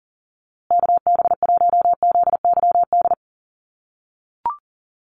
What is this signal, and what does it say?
Morse code, heard as a keyed tone, about 40 wpm, 700 Hz. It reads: KB1ZYD
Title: USA Call Signs Repeated 3x 40wpm